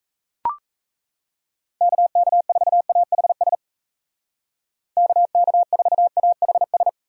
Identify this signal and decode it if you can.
Morse code, heard as a keyed tone, about 33 wpm, 700 Hz. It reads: KK4AHS KK4AHS
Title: Speed Racing - USA Call Signs 28wpm